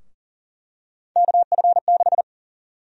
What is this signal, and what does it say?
Morse code, heard as a keyed tone, about 40 wpm, 700 Hz. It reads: KF6
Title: USA Call Sign Prefixes - 40wpm